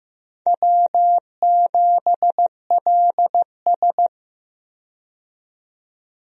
Morse code, heard as a keyed tone, about 15 wpm, 700 Hz. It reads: W7LS
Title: USA Call Signs - 4 Characters - 15wpm